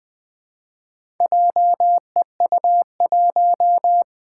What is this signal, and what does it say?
Morse code, heard as a keyed tone, about 20 wpm, 700 Hz. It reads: JEU1
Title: Four Challenging Letter-Number Combinations 20wpm